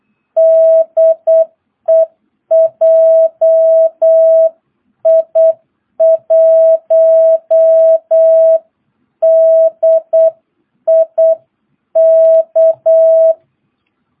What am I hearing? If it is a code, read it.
Morse code, heard as a keyed tone, about 8 wpm, 650 Hz. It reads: DE JI1DIK
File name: jh8cbh / 2022cw